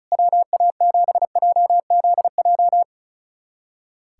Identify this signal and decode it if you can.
Morse code, heard as a keyed tone, about 34 wpm, 700 Hz. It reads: WA7JZJ
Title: USA Callsign DE Callsign - 6 Characters - 35wpm